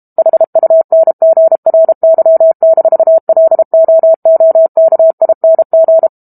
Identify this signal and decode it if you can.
Morse code, heard as a keyed tone, about 32 wpm, 650 Hz. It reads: HUNGRY-LOOKING
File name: Book Two / 06-Chapter 6 - 32 WPM